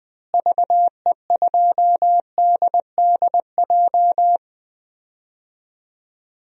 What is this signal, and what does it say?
Morse code, heard as a keyed tone, about 20 wpm, 700 Hz. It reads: VE2DDJ
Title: DXCC-100 Call Signs - 6 Characters - Repeated 3x - 20wpm